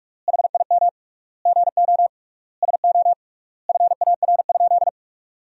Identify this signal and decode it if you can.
Morse code, heard as a keyed tone, about 46 wpm, 700 Hz. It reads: HIM GO SO FAR?
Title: 3x Word-Spacing - Sentences from Top 300 Words - 45wpm